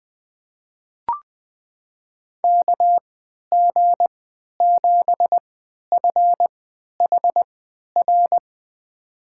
Morse code, heard as a keyed tone, about 20 wpm, spaced as for 11 wpm, 700 Hz. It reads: KG7FHR
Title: USA Call Signs - 6 Characters - 3x Character Spacing - 20wpm